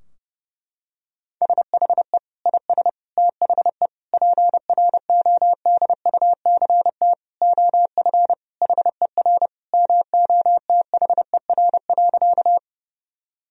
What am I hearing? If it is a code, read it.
Morse code, heard as a keyed tone, about 30 wpm, 700 Hz. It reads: SHE IS THE PRODUCT OF HER MOTHER.
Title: Sentences from Top 1000 Words - Review Part 2 - 30wpm